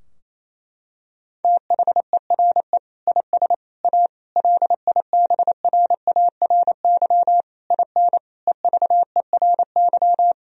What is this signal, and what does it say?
Morse code, heard as a keyed tone, about 28 wpm, 700 Hz. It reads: THERE IS A LIBRARY IN EVERY
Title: Sentences from Top 1000 Words - Review Part 4 - 28wpm